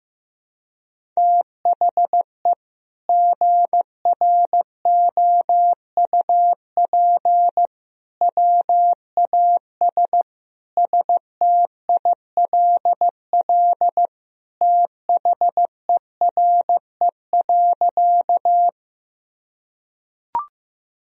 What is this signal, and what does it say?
Morse code, heard as a keyed tone, about 15 wpm, 700 Hz. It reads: THE GROUP WAS STILL THERE.
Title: Sentences from Top 1000 Words - Review Part 1 - 15wpm